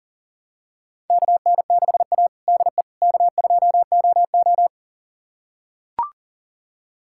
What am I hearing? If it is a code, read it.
Morse code, heard as a keyed tone, about 40 wpm, 700 Hz. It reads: KN6A DE K2OO
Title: USA Callsign DE Callsign - 4 Characters - 40wpm